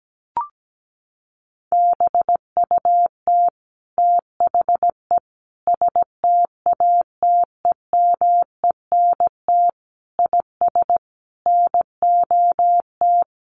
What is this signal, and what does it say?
Morse code, heard as a keyed tone, about 17 wpm, 700 Hz. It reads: BUT THE STATEMENT IS NOT
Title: Sentences from Top 1200 Words 17wpm